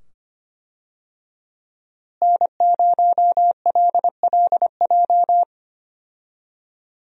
Morse code, heard as a keyed tone, about 25 wpm, 700 Hz. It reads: N0LLJ
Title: USA Call Signs - 5 Characters - Repeated 2x - 25wpm